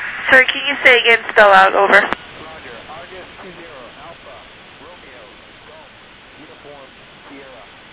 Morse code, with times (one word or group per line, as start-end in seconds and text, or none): none